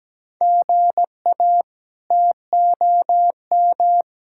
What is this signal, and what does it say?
Morse code, heard as a keyed tone, about 17 wpm, 700 Hz. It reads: GA TOM
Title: K1USN SST QSO Practice - 17wpm